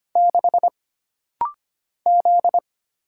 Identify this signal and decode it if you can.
Morse code, heard as a keyed tone, about 25 wpm, 700 Hz. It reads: 6 Z
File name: Single Character (Letter:Number:Punctuation:Prosign) - Rapid-Fire - 25wpm